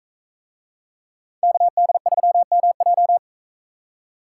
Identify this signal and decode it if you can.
Morse code, heard as a keyed tone, about 40 wpm, 700 Hz. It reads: KD3MJ